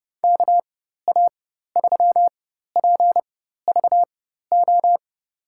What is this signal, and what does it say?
Morse code, heard as a keyed tone, about 31 wpm, spaced as for 14 wpm, 700 Hz. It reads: KA3PVO
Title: USA Call Signs - 6 Characters - 4x Character Spacing - 30wpm